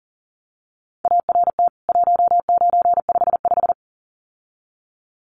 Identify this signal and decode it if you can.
Morse code, heard as a keyed tone, about 40 wpm, 700 Hz. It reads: ART 1955
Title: CWT Exchange 40wpm